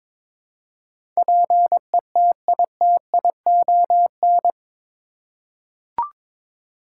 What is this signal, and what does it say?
Morse code, heard as a keyed tone, about 21 wpm, 700 Hz. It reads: PETITION
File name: practice / 22wpm / Eight Letter Words - 22wpm